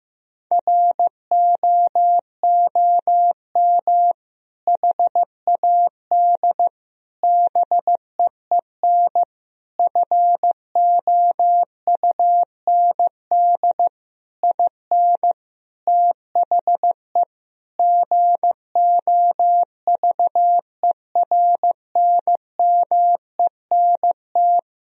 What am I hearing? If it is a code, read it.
Morse code, heard as a keyed tone, about 16 wpm, 700 Hz. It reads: ROOM HAD BEEN FOUND IN THE GOVERNMENT
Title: Sentences from Top 1000 Words - Review Part 3 - 15wpm